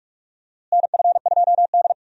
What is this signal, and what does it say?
Morse code, heard as a keyed tone, about 42 wpm, 700 Hz. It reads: NF2D
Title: USA Call Signs - 4 Characters - Repeated 3x - 45wpm